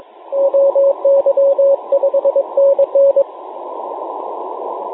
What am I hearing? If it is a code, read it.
Morse code, heard as a keyed tone, about 23 wpm, 550 Hz, with radio noise. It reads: OY 5NN